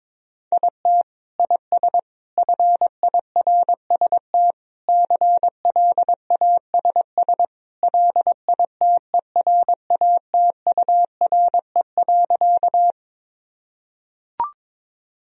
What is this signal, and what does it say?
Morse code, heard as a keyed tone, about 22 wpm, 700 Hz. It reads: IT IS FIRST CLASS LITERATURE.
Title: Sentences from Top 1000 Words - Review Part 3 - 22wpm